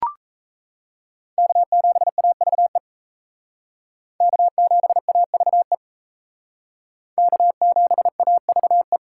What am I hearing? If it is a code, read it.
Morse code, heard as a keyed tone, about 34 wpm, 700 Hz. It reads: K7AVE K7AVE K7AVE
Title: Speed Racing - USA 1x3 Call Signs 28wpm